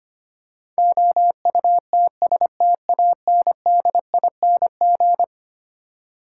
Morse code, heard as a keyed tone, about 25 wpm, 700 Hz. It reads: OUTSTANDING
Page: morse-code-ninja-practice-25wpm